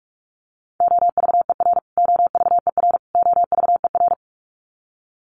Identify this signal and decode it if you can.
Morse code, heard as a keyed tone, about 44 wpm, 700 Hz. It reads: OVER OVER OVER
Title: Top 200 Words Repeated 45wpm